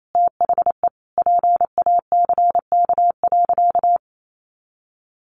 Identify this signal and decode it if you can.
Morse code, heard as a keyed tone, about 28 wpm, 700 Hz. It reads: THE PACK.
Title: Sentences from Top 1300 Words 28wpm